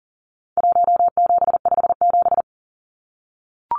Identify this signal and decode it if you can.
Morse code, heard as a keyed tone, about 40 wpm, 700 Hz. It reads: 1757
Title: Numbers - Contest Style - 4-Digit 40wpm